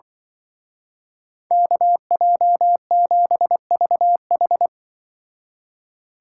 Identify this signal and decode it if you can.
Morse code, heard as a keyed tone, about 23 wpm, 700 Hz. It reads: KJ7VH
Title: Speed Racing - USA 2x2 Call Signs 20wpm